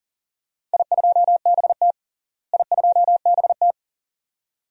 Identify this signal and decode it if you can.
Morse code, heard as a keyed tone, about 40 wpm, 700 Hz. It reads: I2BT I2BT
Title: DXCC-100 Call Signs - 4 Characters - Repeated 2x - 40wpm